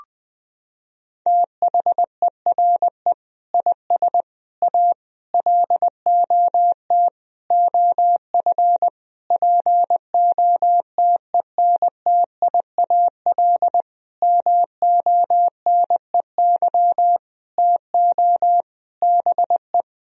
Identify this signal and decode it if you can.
Morse code, heard as a keyed tone, about 20 wpm, 700 Hz. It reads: THERE IS A LOT OF POTENTIAL MONEY TO BE
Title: Sentences from Top 1000 Words - Review Part 4 - 20wpm